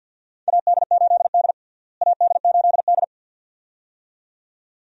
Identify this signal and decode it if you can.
Morse code, heard as a keyed tone, about 46 wpm, 700 Hz. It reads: AD8D AD8D
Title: USA Call Signs - 4 Characters - Repeated 2x - 50wpm